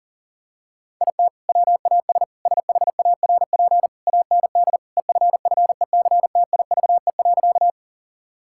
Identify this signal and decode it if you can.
Morse code, heard as a keyed tone, about 40 wpm, 700 Hz. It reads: IT WAS SHARP AND EFFECTIVE.
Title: Sentences from Top 1600 Words 40wpm